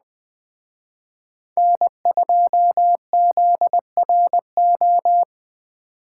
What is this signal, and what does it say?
Morse code, heard as a keyed tone, about 20 wpm, 700 Hz. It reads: N2ZRO
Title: Speed Racing - USA 1x3 Call Signs 17wpm